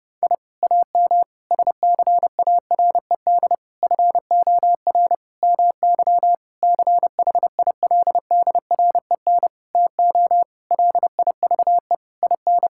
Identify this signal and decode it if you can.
Morse code, heard as a keyed tone, about 30 wpm, 700 Hz. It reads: I AM SCARED FOR MY CHILDREN TO LIVE IN
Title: Sentences from Top 1000 Words - Review Part 5 - 30wpm